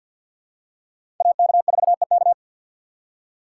Morse code, heard as a keyed tone, about 51 wpm, 700 Hz. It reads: AK4EK
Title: USA Call Signs - 5 Characters - 4x Character Spacing - 50wpm